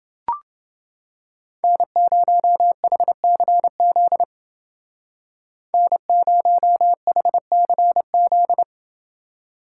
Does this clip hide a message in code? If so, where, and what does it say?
Yes: Morse code from 1.6 to 8.6 s, heard as a keyed tone, about 28 wpm, 700 Hz. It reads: N0HCZ N0HCZ